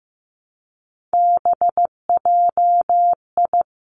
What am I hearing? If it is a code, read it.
Morse code, heard as a keyed tone, about 15 wpm, 700 Hz. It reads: BJI
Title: Three Challenging Letter-Number Combinations 15wpm